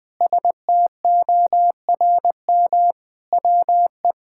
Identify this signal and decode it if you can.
Morse code, heard as a keyed tone, about 20 wpm, 700 Hz. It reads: STORM WE
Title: Sentences with 5-Word Phrases - 20wpm